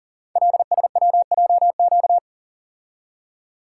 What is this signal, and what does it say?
Morse code, heard as a keyed tone, about 39 wpm, 700 Hz. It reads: LSWJQ
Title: Code Groups - 5 Letter:Numbers - 40wpm